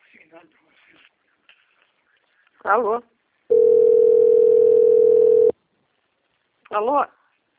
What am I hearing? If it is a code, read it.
Ringback tone (North America), 440 + 480 Hz, with speech.
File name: out- - - - - .453